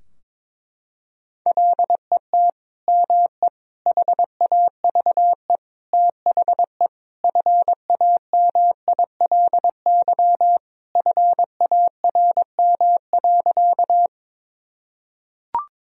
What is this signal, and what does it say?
Morse code, heard as a keyed tone, about 22 wpm, 700 Hz. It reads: LET ME HAVE THE FAMILY FARM.
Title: Sentences from Top 1000 Words - Review Part 2 - 22wpm